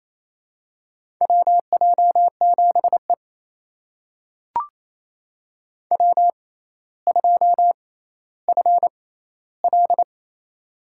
Morse code, heard as a keyed tone, about 28 wpm, spaced as for 14 wpm, 700 Hz. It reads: WJ7E W2FL
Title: USA Call Signs - 4 Characters - 6x Character Spacing - 28wpm